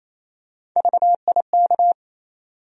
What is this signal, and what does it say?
Morse code, heard as a keyed tone, about 28 wpm, 700 Hz. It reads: VIK
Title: USA Call Sign Postfixes - 3 Letters 28wpm